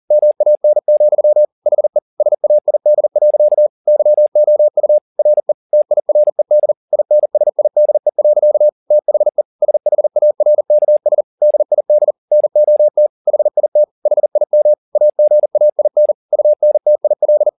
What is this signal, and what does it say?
Morse code, heard as a keyed tone, about 40 wpm, 600 Hz. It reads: MAN, HE SAID. YOU RE TIRED INSIDE. THE SHARKS DID NOT HIT HIM AGAIN UNTIL
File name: the old man and the sea 2 40wpm